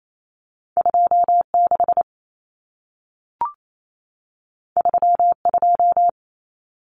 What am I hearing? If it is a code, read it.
Morse code, heard as a keyed tone, about 28 wpm, 700 Hz. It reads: 26 32
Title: Numbers - 2-Digit 28wpm